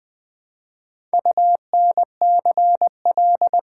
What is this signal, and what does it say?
Morse code, heard as a keyed tone, about 20 wpm, 700 Hz. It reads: UNCL
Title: Build A Word - 1 Letter At A Time - 5-Letter Words - 20wpm